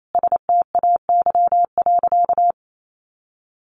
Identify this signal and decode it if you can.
Morse code, heard as a keyed tone, about 28 wpm, 700 Hz. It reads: STAY.
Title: Sentences from Top 1000 Words 28wpm